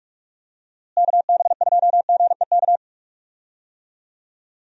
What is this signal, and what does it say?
Morse code, heard as a keyed tone, about 42 wpm, 700 Hz. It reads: KB2GEK
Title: USA Call Signs - 6 Characters - Repeated 2x - 45wpm